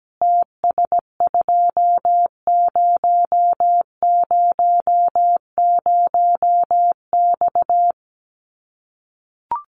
Morse code, heard as a keyed tone, about 17 wpm, 700 Hz. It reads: TS2000X
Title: Common QSO Phrases 17wpm